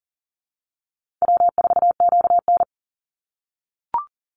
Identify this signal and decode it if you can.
Morse code, heard as a keyed tone, about 40 wpm, 700 Hz. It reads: W4QN
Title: Easy Call Signs - 40wpm